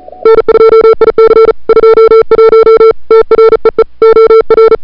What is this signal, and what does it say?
Morse code, heard as a keyed tone, about 37 wpm, 440 Hz. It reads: N2IC 21 TREE OR
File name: CWSprint / February 2007 CW Sprint / 01-2007 CW Sprint Hour1